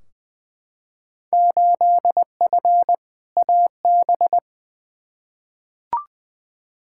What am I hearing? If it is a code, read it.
Morse code, heard as a keyed tone, about 20 wpm, 700 Hz. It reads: 8F AB
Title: Field Day Exchanges Encore - 20wpm